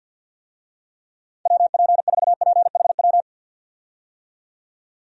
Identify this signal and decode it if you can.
Morse code, heard as a keyed tone, about 46 wpm, 700 Hz. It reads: WP4PHW
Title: USA Call Signs - 6 Characters - 50wpm